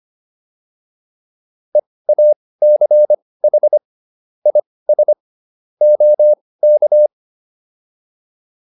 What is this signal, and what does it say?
Morse code, heard as a keyed tone, about 25 wpm, spaced as for 18 wpm, 600 Hz. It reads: EACH IS OK